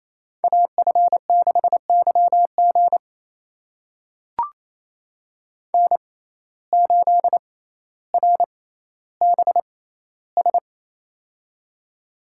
Morse code, heard as a keyed tone, about 28 wpm, spaced as for 14 wpm, 700 Hz. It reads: AF6YG N8RBS
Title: USA Call Signs - 5 Characters - 6x Character Spacing - 28wpm